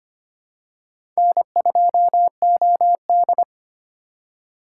Morse code, heard as a keyed tone, about 25 wpm, 700 Hz. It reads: N2OD
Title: USA Call Signs - 4 Characters - 4x Character Spacing - 25wpm